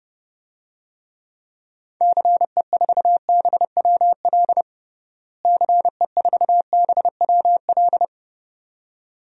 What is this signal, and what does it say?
Morse code, heard as a keyed tone, about 32 wpm, 700 Hz. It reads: CE4BWL CE4BWL
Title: DXCC-100 Call Signs - 6 Characters - Repeated 2x - 30wpm